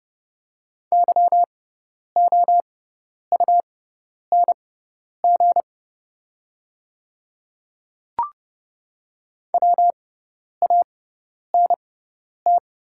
Morse code, heard as a keyed tone, about 31 wpm, spaced as for 10 wpm, 690 Hz. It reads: YOUNG WANT
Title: 6x Character Spacing - Top 500 Words - 30wpm